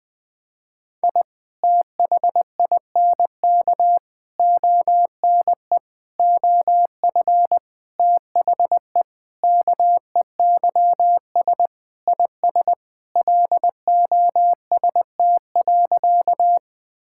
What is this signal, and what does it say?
Morse code, heard as a keyed tone, about 20 wpm, 700 Hz. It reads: I THINK ONE OF THE KEYS IS LOST.
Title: Sentences with 5-Word Phrases - 20wpm